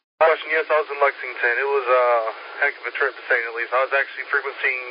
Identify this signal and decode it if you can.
Emergency Alert System (SAME) tones, heard as harsh buzzing bursts, with speech. It is heard from 1.1 to 4.5 s.